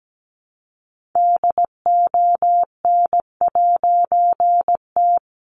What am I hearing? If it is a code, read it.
Morse code, heard as a keyed tone, about 17 wpm, 700 Hz. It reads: DON'T
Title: A Sound Of Thunder 17wpm